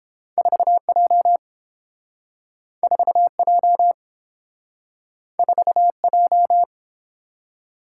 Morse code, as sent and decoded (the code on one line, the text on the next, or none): ....- .--- / ....- .--- / ....- .---
4J 4J 4J